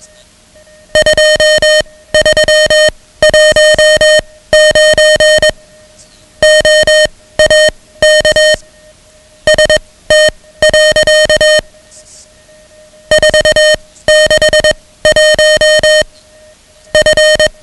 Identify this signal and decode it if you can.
Morse code, heard as a keyed tone, about 21 wpm, spaced as for 16 wpm, 620 Hz, with noise. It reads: 2319 OAK ST. 461 F
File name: ham radio / CodeQuick / Code Quick 10